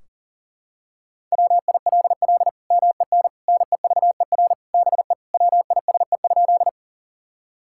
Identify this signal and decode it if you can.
Morse code, heard as a keyed tone, about 37 wpm, 700 Hz. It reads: WILL MEN NEVER BE WISE?
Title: Sentences from Top 1600 Words 40wpm